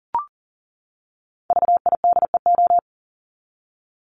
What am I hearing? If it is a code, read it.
Morse code, heard as a keyed tone, about 40 wpm, 700 Hz. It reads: VIDEO
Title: Top 501-750 Words 40wpm